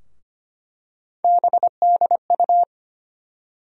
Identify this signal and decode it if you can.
Morse code, heard as a keyed tone, about 25 wpm, 700 Hz. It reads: BDU
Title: USA Call Sign Postfixes - 3 Letters 25wpm